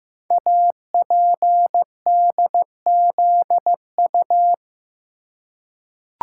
Morse code, heard as a keyed tone, about 15 wpm, 700 Hz. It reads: APDZU
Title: Code Groups - 5 Letters - 15wpm